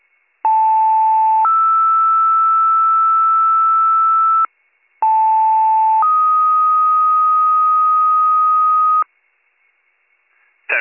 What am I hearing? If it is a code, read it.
Two-tone page, heard as a one-second tone, then a three-second tone, about 880 then 1320 Hz, about 880 then 1250 Hz. It is heard from 0.4 to 9.0 s.